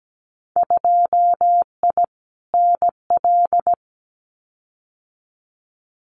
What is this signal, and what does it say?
Morse code, heard as a keyed tone, about 17 wpm, 700 Hz. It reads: 2I NL
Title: Winter Field Day Exchanges Encore 17wpm